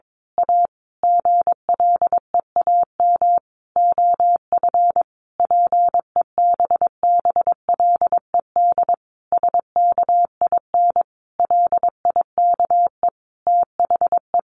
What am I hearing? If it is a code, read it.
Morse code, heard as a keyed tone, about 22 wpm, 700 Hz. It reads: A GLEAM OF PEBBLED SKIN LIKE THE
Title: A Sound Of Thunder 22wpm